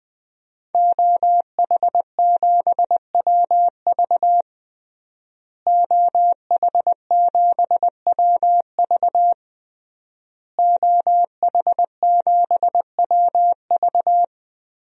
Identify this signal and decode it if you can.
Morse code, heard as a keyed tone, about 20 wpm, 700 Hz. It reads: OH7WV OH7WV OH7WV
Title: DXCC-100 Call Signs - 5 Characters - Repeated 3x - 20wpm